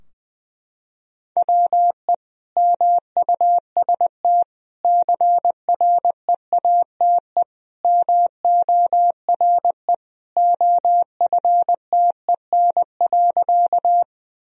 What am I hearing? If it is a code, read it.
Morse code, heard as a keyed tone, about 20 wpm, 700 Hz. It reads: WE MUST CREATE MORE OFTEN.
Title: Sentences from Top 1000 Words - Review Part 3 - 20wpm